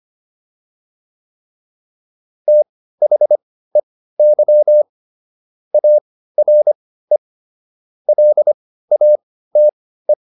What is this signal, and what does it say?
Morse code, heard as a keyed tone, about 25 wpm, spaced as for 15 wpm, 600 Hz. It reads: THEY ARE LATE
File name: PR305 15